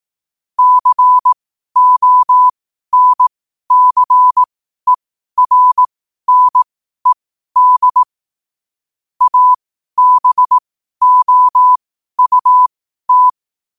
Morse code, heard as a keyed tone, about 18 wpm, spaced as for 12 wpm, 1000 Hz. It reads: CONCERNED ABOUT
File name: qotdfiles / 12wpm